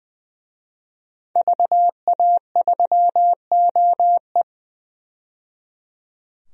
Morse code, heard as a keyed tone, about 20 wpm, 700 Hz. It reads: VA3OE
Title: DXCC-100 Call Signs - 5 Characters - 20wpm